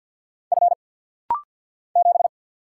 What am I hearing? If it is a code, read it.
Morse code, heard as a keyed tone, about 50 wpm, 700 Hz. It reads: F 7
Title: Single Letter-Number - Rapid-Fire 50wpm